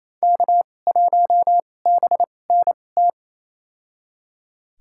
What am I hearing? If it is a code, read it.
Morse code, heard as a keyed tone, about 28 wpm, spaced as for 20 wpm, 700 Hz. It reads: K1BNT
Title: USA Call Signs - 5 Characters - 2x Character Spacing - 28wpm